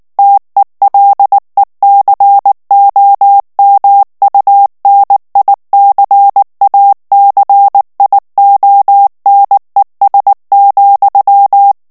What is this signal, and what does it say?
Morse code, heard as a keyed tone, about 19 wpm, 800 Hz. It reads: TELECOMUNICACIONES,
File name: morse mp3 / ESPAGNOL TEXTE EN CLAIR 2 VITESSE 18 MOTS MINUTES